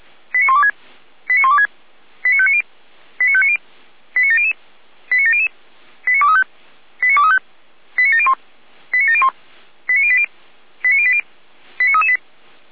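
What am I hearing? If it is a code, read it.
Five-tone selective call (ZVEI), heard as quick runs of five tones, tones of about 70 ms. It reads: 89126 89126 89590 89590 89700 89700 89235 89235 89701 89701 89080 89080 89338
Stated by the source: ZVEI five-tone call 89115 received by SDR.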